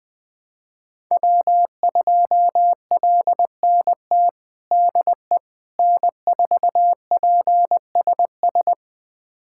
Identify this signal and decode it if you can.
Morse code, heard as a keyed tone, about 21 wpm, 700 Hz. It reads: W2LNT DE N4PSS
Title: USA Callsign DE Callsign - 5 Characters - 20wpm